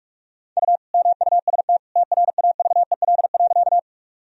Morse code, heard as a keyed tone, about 46 wpm, 700 Hz. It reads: U MUST TRAVEL.
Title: Sentences from Top 400 Words and Abbreviations 45wpm